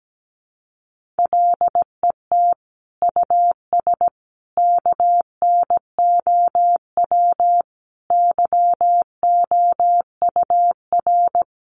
Morse code, heard as a keyed tone, about 17 wpm, 700 Hz. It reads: LET US KNOW YOUR
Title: Sentences from Top 1400 Words 17wpm